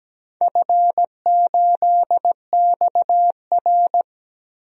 Morse code, heard as a keyed tone, about 17 wpm, 700 Hz. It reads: F8XR
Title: DXCC-100 Call Signs - 4 Characters - Repeated 2x - 17wpm